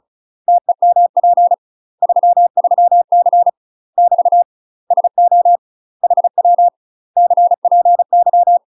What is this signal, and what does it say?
Morse code, heard as a keyed tone, about 34 wpm, 700 Hz. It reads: TEMP 33C <BT> SO HW CPY
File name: UK MASTER QSOs TRAINING / 35WPM / UK MASTER QSOs 0025